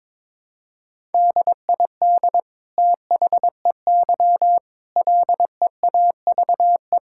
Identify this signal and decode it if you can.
Morse code, heard as a keyed tone, about 22 wpm, 700 Hz. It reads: DID THEY LEAVE
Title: Sentences from Top 1000 Words - Review Part 5 - 22wpm